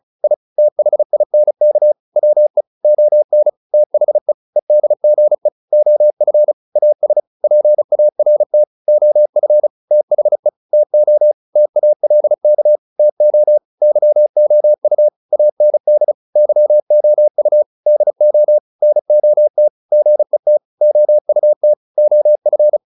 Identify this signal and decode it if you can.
Morse code, heard as a keyed tone, about 34 wpm, 600 Hz. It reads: I THINK WE ON THE EDGE OF AS PART OF THE TO TALK TO YOU AND YOU DO NOT GET OUT OF